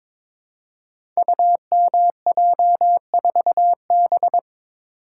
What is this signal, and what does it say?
Morse code, heard as a keyed tone, about 22 wpm, 700 Hz. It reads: UMJ4B
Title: Code Groups - 5 Letter:Numbers - 22wpm